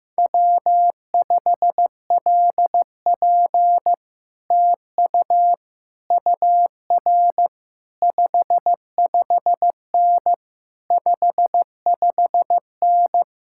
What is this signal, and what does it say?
Morse code, heard as a keyed tone, about 15 wpm, 700 Hz. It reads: W5LP TU UR 55N 55N
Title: Speed-Racing Common QSO Phrases Encore - 15wpm